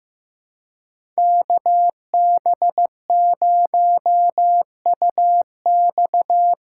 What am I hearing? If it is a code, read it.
Morse code, heard as a keyed tone, about 15 wpm, 700 Hz. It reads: KB0UX
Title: USA Call Signs - 3x Character Spacing - 15wpm